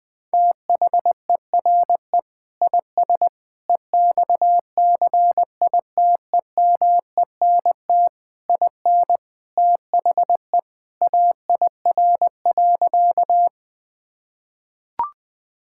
Morse code, heard as a keyed tone, about 21 wpm, 700 Hz. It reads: THERE IS EXCITEMENT IN THE AIR.
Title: Sentences with 3-Word Phrases - 20wpm